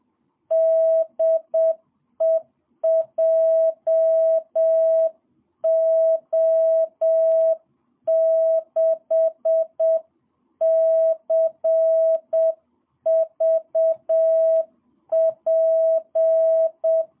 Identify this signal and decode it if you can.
Morse code, heard as a keyed tone, about 7 wpm, 660 Hz. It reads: DE JO6CVP
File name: jh8cbh / 2022cw